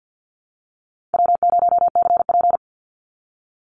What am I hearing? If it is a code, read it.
Morse code, heard as a keyed tone, about 50 wpm, 700 Hz. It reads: W0CP
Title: Easy Call Signs - 50wpm